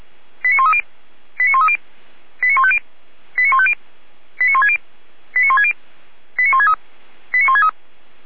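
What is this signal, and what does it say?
Five-tone selective call (ZVEI), heard as quick runs of five tones, tones of about 80 ms. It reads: 89130 89130 89150 89150 89160 89160 89163 89163
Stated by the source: ZVEI five-tone selective call, code 89120, received by SDR.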